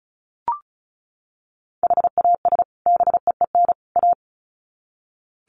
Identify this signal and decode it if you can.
Morse code, heard as a keyed tone, about 34 wpm, 700 Hz. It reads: HAS BEEN A
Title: Sets of 3 Words 35wpm